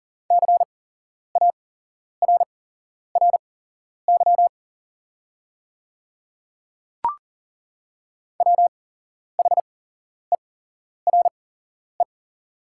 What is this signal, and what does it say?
Morse code, heard as a keyed tone, about 40 wpm, spaced as for 11 wpm, 700 Hz. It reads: CARRY WHERE
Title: 8x Character Spacing - Top 500 Words - 40wpm